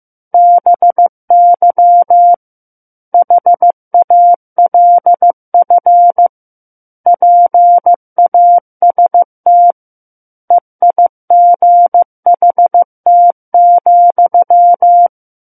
Morse code, heard as a keyed tone, about 15 wpm, 700 Hz. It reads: BY HALF PAST EIGHT,
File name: Book One / 06-Chapter 6 - 15 WPM